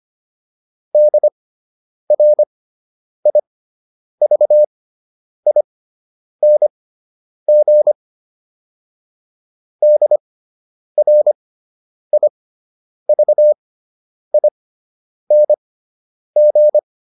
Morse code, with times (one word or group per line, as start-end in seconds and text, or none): 0.9-7.9 DRIVING
9.8-16.8 DRIVING